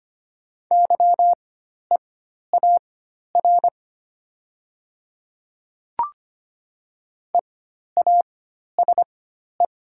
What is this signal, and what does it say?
Morse code, heard as a keyed tone, about 25 wpm, spaced as for 12 wpm, 690 Hz. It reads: YEAR EASE
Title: 4x Character Spacing - Top 300 Words - 25wpm